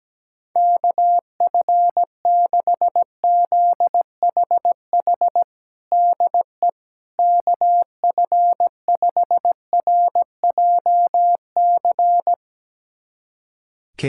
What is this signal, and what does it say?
Morse code, heard as a keyed tone, about 17 wpm, 700 Hz. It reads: KF6ZHH DE KF5RJC
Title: USA Callsign DE Callsign 17wpm